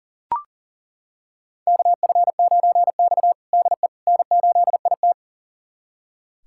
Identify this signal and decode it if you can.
Morse code, heard as a keyed tone, about 40 wpm, 700 Hz. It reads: KF9X DE N8IT
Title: USA Callsign DE Callsign - 4 Characters - 40wpm